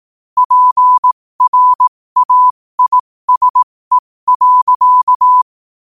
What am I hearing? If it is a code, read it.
Morse code, heard as a keyed tone, about 18 wpm, spaced as for 15 wpm, 1000 Hz. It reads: PRAISE.
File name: qotdfiles / 15wpm